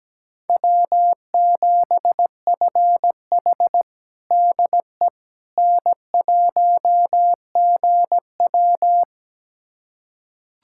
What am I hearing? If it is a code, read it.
Morse code, heard as a keyed tone, about 17 wpm, 700 Hz. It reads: W7FH DE N1GW
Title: USA Callsign DE Callsign - 4 Characters - 17wpm